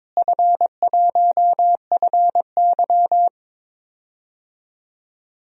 Morse code, heard as a keyed tone, about 22 wpm, 700 Hz. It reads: F1FY
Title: DXCC-100 Call Signs - 4 Characters - Repeated 3x - 22wpm